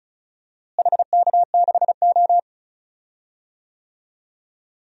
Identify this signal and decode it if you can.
Morse code, heard as a keyed tone, about 34 wpm, 700 Hz. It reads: HK6O
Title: DXCC-100 Call Signs - 4 Characters - Repeated 3x - 35wpm